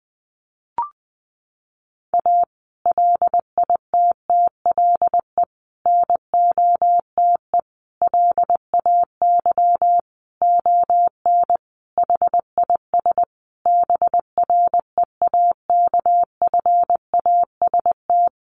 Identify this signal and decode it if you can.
Morse code, heard as a keyed tone, about 20 wpm, 700 Hz. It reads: A LITTLE NOTE LAY ON HIS BREAKFAST
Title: Sentences from Top 1500 Words 20wpm